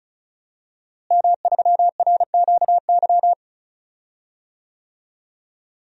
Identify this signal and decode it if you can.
Morse code, heard as a keyed tone, about 34 wpm, 700 Hz. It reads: M3RQY
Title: DXCC-100 Call Signs - 5 Characters - Repeated 2x - 35wpm